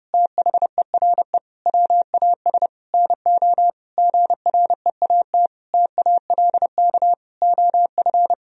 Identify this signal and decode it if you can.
Morse code, heard as a keyed tone, about 30 wpm, 700 Hz. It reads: THERE WAS NO GREAT TALK OF
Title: Sentences from Top 1000 Words - Review Part 2 - 30wpm